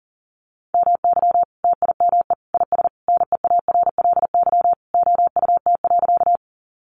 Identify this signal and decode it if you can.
Morse code, heard as a keyed tone, about 40 wpm, 700 Hz. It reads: MY TIME IS NEARLY OUT.
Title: Sentences from Top 1300 Words 40wpm